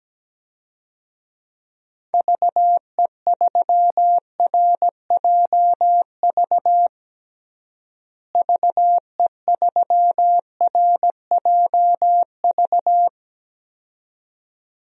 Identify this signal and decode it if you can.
Morse code, heard as a keyed tone, about 17 wpm, 700 Hz. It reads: VE3RJV VE3RJV
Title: DXCC-100 Call Signs - 6 Characters - Repeated 2x - 17wpm